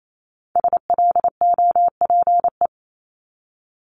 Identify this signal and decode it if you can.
Morse code, heard as a keyed tone, about 28 wpm, 700 Hz. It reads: SLOPE
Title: Multi-letter Phonemes Repeated 3x 28wpm